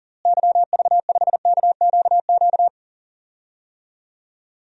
Morse code, heard as a keyed tone, about 40 wpm, 700 Hz. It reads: YV5KQQ
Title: DXCC-100 Call Signs - 6 Characters - Repeated 3x - 40wpm